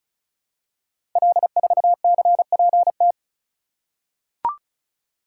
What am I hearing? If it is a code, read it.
Morse code, heard as a keyed tone, about 36 wpm, 700 Hz. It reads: L4CPT
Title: Code Groups - 5 Letter:Numbers - 35wpm